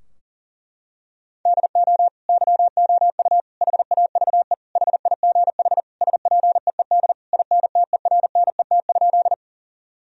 Morse code, heard as a keyed tone, about 37 wpm, 700 Hz. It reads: DO YOU HAVE HIGH SPEED INTERNET?
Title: Sentences from Top 1000 Words - Review Part 4 - 40wpm